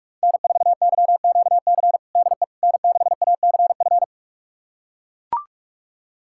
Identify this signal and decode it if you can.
Morse code, heard as a keyed tone, about 41 wpm, 700 Hz. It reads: N4YQC DE N6ACF
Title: USA Callsign DE Callsign - 5 Characters - 45wpm